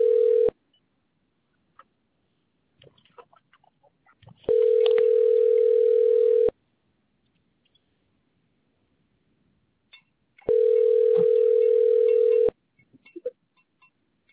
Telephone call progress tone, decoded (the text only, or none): ringback tone